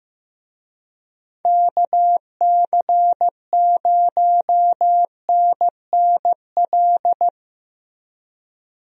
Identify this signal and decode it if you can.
Morse code, heard as a keyed tone, about 16 wpm, 700 Hz. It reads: KC0NNL
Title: USA Call Signs - 6 Characters - Repeated 3x - 15wpm